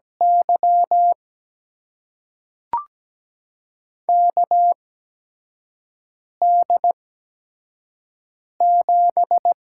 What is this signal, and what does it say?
Morse code, heard as a keyed tone, about 17 wpm, spaced as for 5 wpm, 700 Hz. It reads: Y KD7
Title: USA Call Signs - 5 Characters - 8x Character Spacing - 17wpm